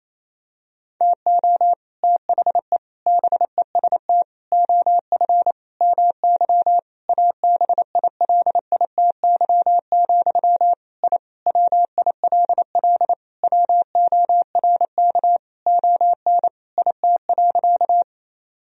Morse code, heard as a keyed tone, about 28 wpm, 700 Hz. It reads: TO THE BEST OF MY ABILITY, I WILL WORK ON IT.
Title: Sentences from Top 1000 Words - Review Part 3 - 28wpm